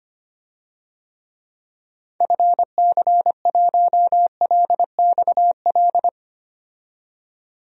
Morse code, heard as a keyed tone, about 25 wpm, 700 Hz. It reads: FC1LXL
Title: DXCC-100 Call Signs - 6 Characters - 25wpm